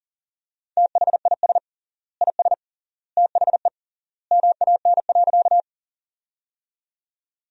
Morse code, heard as a keyed tone, about 40 wpm, 700 Hz. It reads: THIS IS THE MAN.
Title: 3x Word-Spacing - Sentences from Top 200 Words - 40wpm